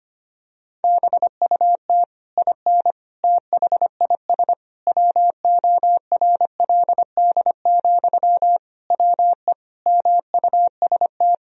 Morse code, heard as a keyed tone, about 25 wpm, 700 Hz. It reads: BUT IN THIS WORLD, WE MUST
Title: Sentences from Top 1000 Words - Review Part 4 - 25wpm